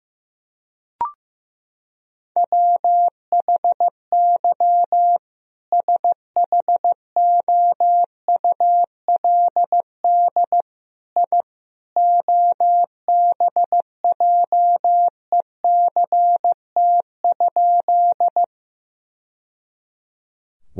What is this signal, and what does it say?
Morse code, heard as a keyed tone, about 15 wpm, 700 Hz. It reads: WHY SHOULD I OBJECT?
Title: Sentences from Top 1000 Words - Review Part 3 - 15wpm